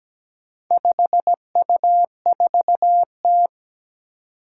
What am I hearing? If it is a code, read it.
Morse code, heard as a keyed tone, about 17 wpm, 700 Hz. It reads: 5U4T
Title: Four Challenging Letter-Number Combinations 17wpm